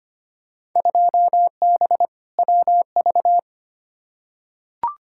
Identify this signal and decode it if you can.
Morse code, heard as a keyed tone, about 25 wpm, 700 Hz. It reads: 2B WV
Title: Field Day Exchanges Encore - 25wpm